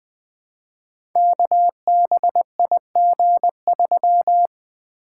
Morse code, heard as a keyed tone, about 20 wpm, 700 Hz. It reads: KBIG3
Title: Code Groups - 5 Letter:Numbers - 20wpm